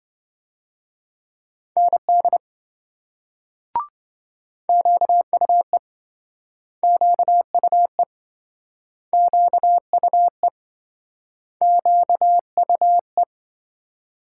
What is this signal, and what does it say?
Morse code, heard as a keyed tone, about 28 wpm, 700 Hz. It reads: ND QUE QUE QUE QUE